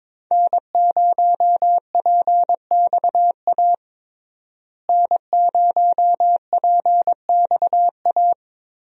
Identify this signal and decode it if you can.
Morse code, heard as a keyed tone, about 22 wpm, 700 Hz. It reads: N0PXA N0PXA
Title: USA Call Signs - 5 Characters - Repeated 3x - 22wpm